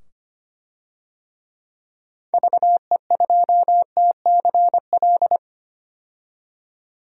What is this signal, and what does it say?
Morse code, heard as a keyed tone, about 25 wpm, 700 Hz. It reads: VE2TCL
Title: DXCC-100 Call Signs - 6 Characters - 25wpm